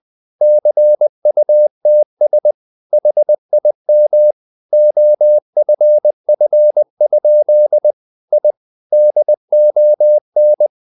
Morse code, heard as a keyed tone, about 20 wpm, 600 Hz. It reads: CUTS HIM OFF? I DON
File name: the old man and the sea / the old man and the sea part 1 20wpm